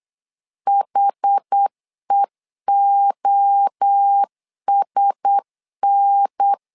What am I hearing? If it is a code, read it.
Morse code, heard as a keyed tone, about 8 wpm, 800 Hz. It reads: HEOSN
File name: LEZIONI / 4 Q1S 8wpm 2710